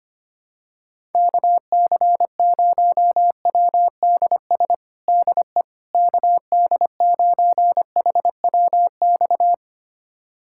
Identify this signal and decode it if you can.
Morse code, heard as a keyed tone, about 25 wpm, 700 Hz. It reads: KC0WDS DE KD9HWX
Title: USA Callsign DE Callsign - 6 Characters - 25wpm